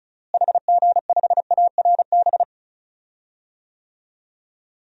Morse code, heard as a keyed tone, about 34 wpm, 700 Hz. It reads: HG5ARB
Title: DXCC-100 Call Signs - 6 Characters - Repeated 2x - 35wpm